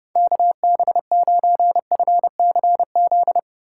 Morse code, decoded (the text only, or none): KB9FCZ